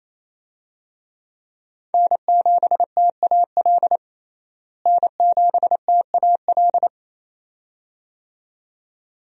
Morse code, heard as a keyed tone, about 28 wpm, 700 Hz. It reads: N7TAL N7TAL
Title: USA Call Signs - 5 Characters - Repeated 2x - 28wpm